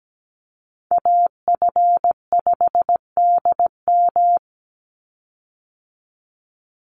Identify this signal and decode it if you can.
Morse code, heard as a keyed tone, about 17 wpm, 700 Hz. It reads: AF5DM
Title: Call Signs Repeated 2x - 17wpm